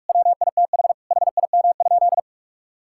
Morse code, heard as a keyed tone, about 46 wpm, 700 Hz. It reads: WITH HIM?
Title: Sentences from Top 1000 Words - Review Part 5 - 45wpm